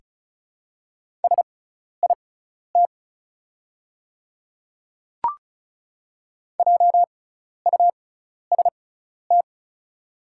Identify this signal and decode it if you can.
Morse code, heard as a keyed tone, about 34 wpm, spaced as for 12 wpm, 690 Hz. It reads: SIT JUST
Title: 6x Character Spacing - Top 500 Words - 35wpm